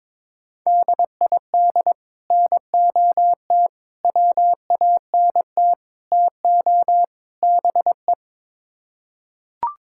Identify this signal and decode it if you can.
Morse code, heard as a keyed tone, about 22 wpm, 700 Hz. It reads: DID NOT WANT TO BE
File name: practice / 22wpm / Build A Phrase - 5-Word Phrases - 22wpm